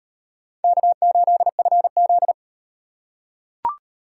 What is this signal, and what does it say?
Morse code, heard as a keyed tone, about 38 wpm, 700 Hz. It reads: K8FZ